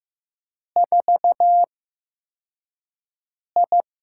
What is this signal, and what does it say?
Morse code, heard as a keyed tone, about 15 wpm, spaced as for 4 wpm, 700 Hz. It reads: 4I
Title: USA Call Signs - 4 Characters - 8x Character Spacing - 15wpm